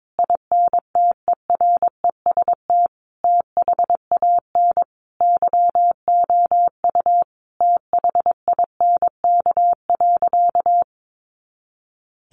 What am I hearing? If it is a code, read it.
Morse code, heard as a keyed tone, about 22 wpm, 700 Hz. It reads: INTEREST THAN YOU THINK.
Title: Sentences from Top 400 Words 22wpm